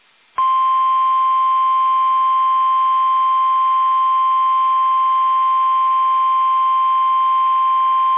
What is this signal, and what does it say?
Emergency Alert System (SAME) tones (the 1050 Hz attention tone). It is heard from 0.4 to 7.6 s.
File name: eas / KUSO / KUSO